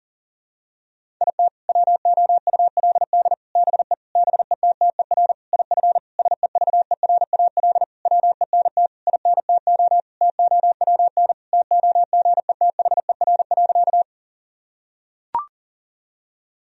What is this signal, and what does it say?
Morse code, heard as a keyed tone, about 40 wpm, 700 Hz. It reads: IT WOULD BE BETTER IF SEVERAL WENT INTO TOWN TOGETHER.
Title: Sentences from Top 1000 Words - Review Part 2 - 40wpm